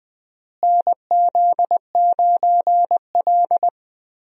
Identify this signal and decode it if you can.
Morse code, heard as a keyed tone, about 20 wpm, 700 Hz. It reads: NZ9L
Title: USA Call Signs - 4 Characters - Repeated 3x - 20wpm